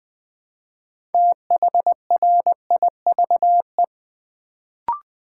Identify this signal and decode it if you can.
Morse code, heard as a keyed tone, about 20 wpm, 700 Hz. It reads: THRIVE
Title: Build A Word - 3 Letters At A Time - 6-Letter Words - 20wpm